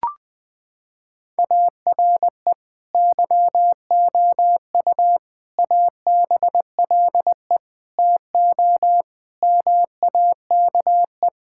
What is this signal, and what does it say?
Morse code, heard as a keyed tone, about 20 wpm, 700 Hz. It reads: ARE YOU ABLE TO MAKE
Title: Sentences from Top 1000 Words - Review Part 6 - 20wpm